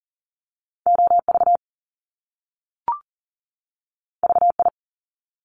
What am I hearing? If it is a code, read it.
Morse code, heard as a keyed tone, about 40 wpm, 700 Hz. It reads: OV VI
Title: Top Two Letter Combinations in Words 40wpm